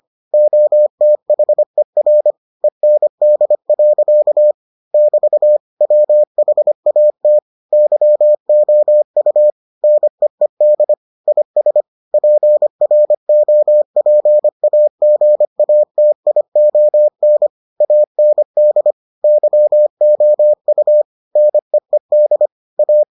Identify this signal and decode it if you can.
Morse code, heard as a keyed tone, about 25 wpm, 600 Hz. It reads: OTHER END. <BT> WHAT YOU NEED IS PROPAGATION AND YOU NEED A